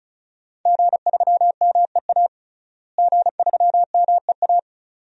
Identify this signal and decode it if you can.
Morse code, heard as a keyed tone, about 36 wpm, 700 Hz. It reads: G3MEA G3MEA
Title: DXCC-100 Call Signs - 5 Characters - Repeated 2x - 35wpm